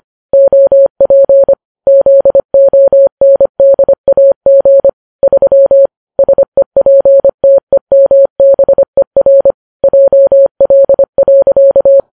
Morse code, heard as a keyed tone, about 25 wpm, 550 Hz. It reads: OP ZONDAG 3 SEPTEMBER JL.